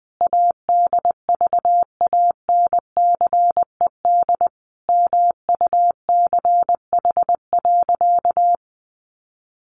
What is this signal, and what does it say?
Morse code, heard as a keyed tone, about 20 wpm, 700 Hz. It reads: ADVANCED MUCH.
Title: Sentences from Top 1200 Words 20wpm